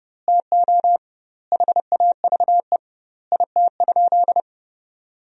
Morse code, heard as a keyed tone, about 30 wpm, 700 Hz. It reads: TO HAVE IT?
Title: 2x Word-Spacing - Sentences from Top 100 Words - 30wpm